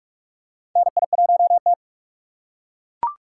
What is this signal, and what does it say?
Morse code, heard as a keyed tone, about 42 wpm, 700 Hz. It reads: NI1T
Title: USA Call Signs - 6x Character Spacing - 45wpm